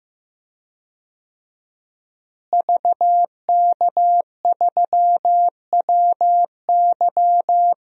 Morse code, heard as a keyed tone, about 15 wpm, 700 Hz. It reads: VK3WY
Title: DXCC-100 Call Signs - 5 Characters - Repeated 2x - 15wpm